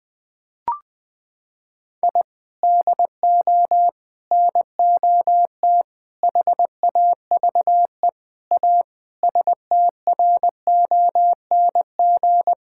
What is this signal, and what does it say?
Morse code, heard as a keyed tone, about 20 wpm, 700 Hz. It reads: I DO NOT HAVE A STRONG
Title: Sentences from Top 1000 Words - Review Part 5 - 20wpm